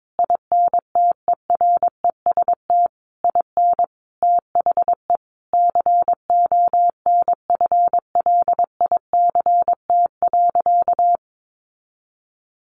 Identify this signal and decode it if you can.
Morse code, heard as a keyed tone, about 22 wpm, 700 Hz. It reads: INTEREST IN THE CONFLICT.
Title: Sentences from Top 1300 Words 22wpm